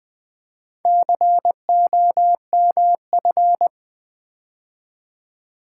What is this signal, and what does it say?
Morse code, heard as a keyed tone, about 20 wpm, 700 Hz. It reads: COMF
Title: Build A Word - 4 Letters At A Time - 11-Letter Words - 20wpm